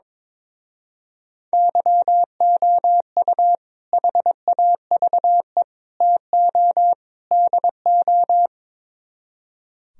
Morse code, heard as a keyed tone, about 22 wpm, 700 Hz. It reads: YOU HAVE TO DO
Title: Speed-Racing - Sets of 4 Words 22wpm